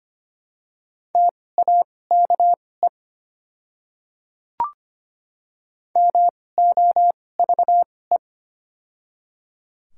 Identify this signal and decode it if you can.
Morse code, heard as a keyed tone, about 25 wpm, spaced as for 18 wpm, 690 Hz. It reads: TAKE MOVE
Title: 2x Character Spacing - Top 200 Words - 25wpm